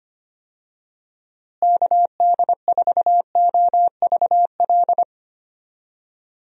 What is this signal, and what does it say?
Morse code, heard as a keyed tone, about 25 wpm, 700 Hz. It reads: KD4OVL
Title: USA Call Signs - 6 Characters - Repeated 3x - 25wpm